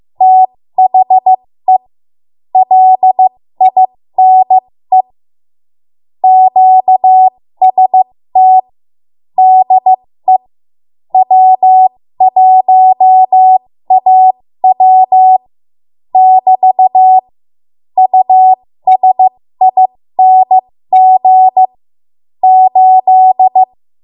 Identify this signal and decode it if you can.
Morse code, heard as a keyed tone, about 15 wpm, spaced as for 13 wpm, 750 Hz. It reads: THE LINE QST DE W1AW <BT> USING 8
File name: Morse / 13WPM